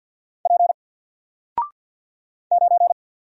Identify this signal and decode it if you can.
Morse code, heard as a keyed tone, about 50 wpm, 700 Hz. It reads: P 9
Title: Single Character (Letter:Number:Punctuation:Prosign) 50wpm